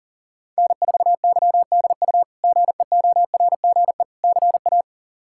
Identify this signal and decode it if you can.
Morse code, heard as a keyed tone, about 40 wpm, 700 Hz. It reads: N4YDU GEORGE CA
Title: K1USN SST QSO Practice - 40wpm